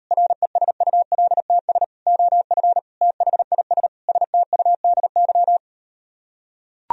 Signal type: Morse code

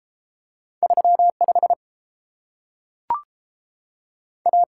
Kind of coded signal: Morse code

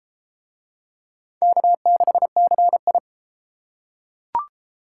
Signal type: Morse code

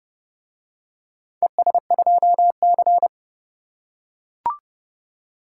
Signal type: Morse code

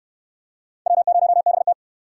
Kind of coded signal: Morse code

